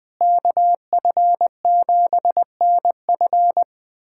Morse code, heard as a keyed tone, about 20 wpm, 700 Hz. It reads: KF7NF
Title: USA Callsign DE Callsign 20wpm